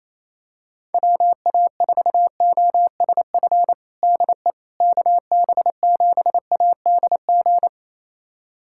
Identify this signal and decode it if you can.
Morse code, heard as a keyed tone, about 28 wpm, 700 Hz. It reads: WA4OSF DE KB7ADG
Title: USA Callsign DE Callsign - 6 Characters - 28wpm